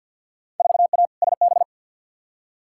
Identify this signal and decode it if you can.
Morse code, heard as a keyed tone, about 47 wpm, 700 Hz. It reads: 4A SB
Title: Field Day Exchanges Encore - 50wpm